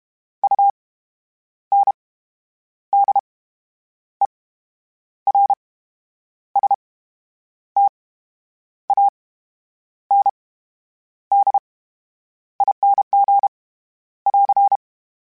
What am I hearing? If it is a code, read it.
Morse code, heard as a keyed tone, about 32 wpm, 800 Hz. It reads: U N D E R S T A N D I N G <AR>
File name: Special INGs / Special INGs-mp3s / VerySlowToFaster Ing @30